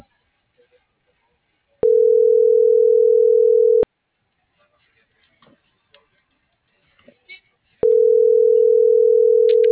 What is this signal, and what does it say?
Ringback tone (North America), 440 + 480 Hz.